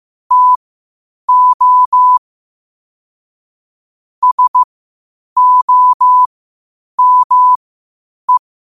Morse code, heard as a keyed tone, about 15 wpm, spaced as for 8 wpm, 1000 Hz. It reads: TO SOME
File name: qotdfiles / 8wpm